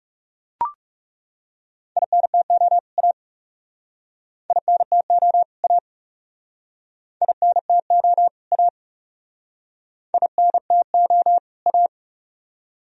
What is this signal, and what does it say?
Morse code, heard as a keyed tone, about 39 wpm, 700 Hz. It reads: INTO A INTO A INTO A INTO A